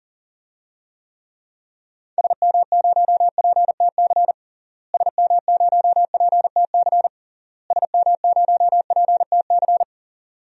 Morse code, heard as a keyed tone, about 40 wpm, 700 Hz. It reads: SM0PTC SM0PTC SM0PTC
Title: DXCC-100 Call Signs - 6 Characters - Repeated 3x - 40wpm